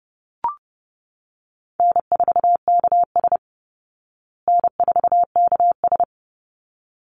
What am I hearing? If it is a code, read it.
Morse code, heard as a keyed tone, about 30 wpm, 700 Hz. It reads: N4KS N4KS
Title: Easy Call Signs Repeated 2x - 30wpm